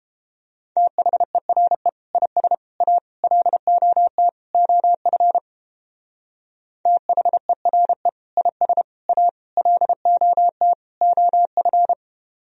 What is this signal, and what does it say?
Morse code, heard as a keyed tone, about 29 wpm, 700 Hz. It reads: THERE IS A LOT OF THERE IS A LOT OF